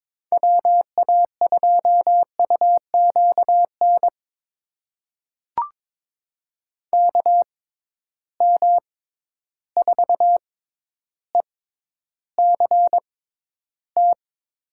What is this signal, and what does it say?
Morse code, heard as a keyed tone, about 21 wpm, spaced as for 11 wpm, 700 Hz. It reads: WA2UQN KM4ECT
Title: USA Call Signs - 6 Characters - 6x Character Spacing - 22wpm